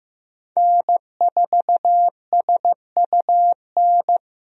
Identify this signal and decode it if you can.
Morse code, heard as a keyed tone, about 16 wpm, 700 Hz. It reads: N4SUN
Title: USA Call Signs - 5 Characters - Repeated 3x - 15wpm